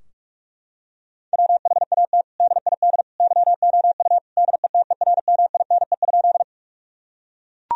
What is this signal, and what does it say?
Morse code, heard as a keyed tone, about 41 wpm, 700 Hz. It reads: WHAT DID YOU DETERMINE?
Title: Sentences from Top 1000 Words - Review Part 4 - 45wpm